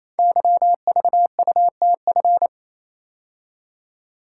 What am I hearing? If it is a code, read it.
Morse code, heard as a keyed tone, about 29 wpm, 700 Hz. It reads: YVUTF
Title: Code Groups - 5 Letter:Numbers - 28wpm